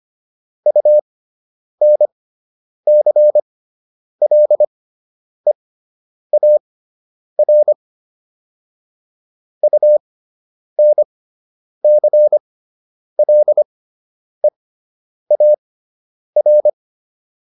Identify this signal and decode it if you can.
Morse code, heard as a keyed tone, about 25 wpm, spaced as for 9 wpm, 600 Hz. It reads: UNCLEAR UNCLEAR